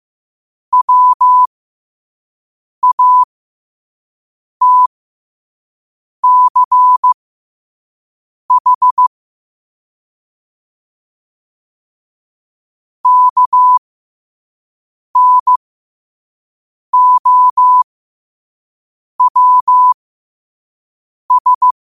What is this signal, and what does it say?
Morse code, heard as a keyed tone, about 15 wpm, spaced as for 5 wpm, 1000 Hz. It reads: WATCH KNOWS
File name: qotdfiles / 5wpm